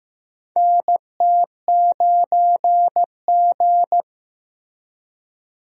Morse code, heard as a keyed tone, about 15 wpm, 700 Hz. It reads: NT9G
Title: USA Call Signs - 4 Characters - Repeated 2x - 15wpm